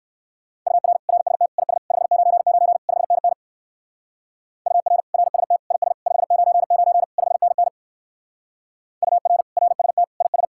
Morse code, heard as a keyed tone, about 60 wpm, 700 Hz. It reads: UR RST IS 599 5NN UR RST IS 599 5NN UR RST IS